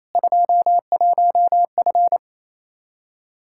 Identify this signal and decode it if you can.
Morse code, heard as a keyed tone, about 28 wpm, 700 Hz. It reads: 21F
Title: Practice QSO - Exchange 2 - Info - 28wpm